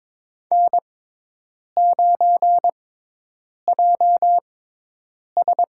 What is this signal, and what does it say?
Morse code, heard as a keyed tone, about 22 wpm, spaced as for 8 wpm, 700 Hz. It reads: N9JS
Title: USA Call Signs - 6x Character Spacing - 22wpm